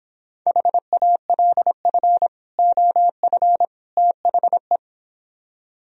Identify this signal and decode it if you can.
Morse code, heard as a keyed tone, about 26 wpm, 700 Hz. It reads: HALF OF THE